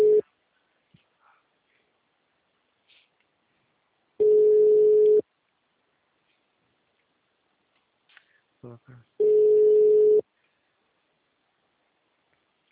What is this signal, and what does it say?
Ringback tone (Europe, 425 Hz), with speech.